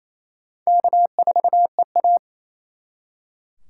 Morse code, heard as a keyed tone, about 35 wpm, 700 Hz. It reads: K4EA
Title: Speed Racing - USA 1x2 Call Signs 28wpm